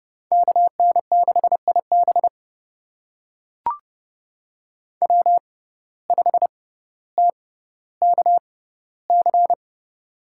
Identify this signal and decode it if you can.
Morse code, heard as a keyed tone, about 32 wpm, spaced as for 16 wpm, 700 Hz. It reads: KN6IB W5TKC
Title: USA Call Signs - 5 Characters - 6x Character Spacing - 30wpm